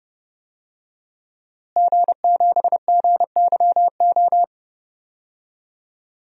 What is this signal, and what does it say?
Morse code, heard as a keyed tone, about 30 wpm, 700 Hz. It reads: G7GYO
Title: DXCC-100 Call Signs - 5 Characters - Repeated 2x - 30wpm